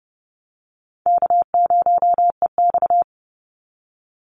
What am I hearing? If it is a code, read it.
Morse code, heard as a keyed tone, about 30 wpm, 700 Hz. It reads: K0EX
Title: Easy Call Signs - 30wpm